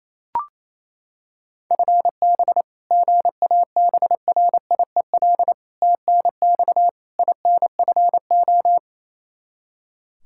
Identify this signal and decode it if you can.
Morse code, heard as a keyed tone, about 28 wpm, 700 Hz. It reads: FB GABRIEL TNX INFO
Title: Practice QSO - Exchange 3 - More Info - 28wpm